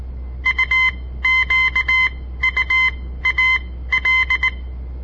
Morse code, heard as a keyed tone, about 19 wpm, spaced as for 14 wpm, 2050 Hz. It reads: UQUAL
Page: On the air Morse code practice from the Southern Montana Amatuer Radio Association; all code, no read back